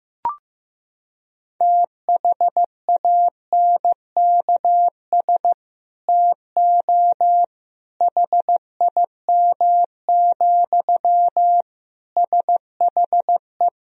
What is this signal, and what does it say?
Morse code, heard as a keyed tone, about 16 wpm, 700 Hz. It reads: THANKS TO HIM, SHE
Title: Sentences from Top 1000 Words - Review Part 3 - 15wpm